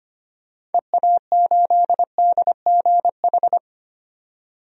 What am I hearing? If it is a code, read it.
Morse code, heard as a keyed tone, about 25 wpm, 700 Hz. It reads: EA8DGH
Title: DXCC-100 Call Signs - 6 Characters - 25wpm